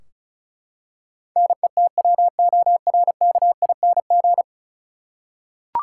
Morse code, heard as a keyed tone, about 34 wpm, 700 Hz. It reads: NETWORKING
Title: Practice QSO - Exchange 3 - More Info - 35wpm